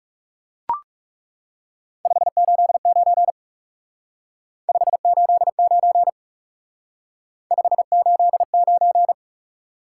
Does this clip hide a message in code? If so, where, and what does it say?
Yes: Morse code from 2.1 to 9.1 s, heard as a keyed tone, about 38 wpm, 700 Hz. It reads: 589 589 589